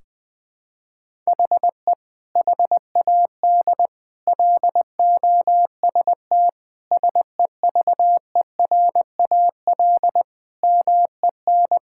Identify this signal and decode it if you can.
Morse code, heard as a keyed tone, about 21 wpm, 700 Hz. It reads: HE HAD LOST SEVERAL MEN
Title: Sentences from Top 1600 Words 20wpm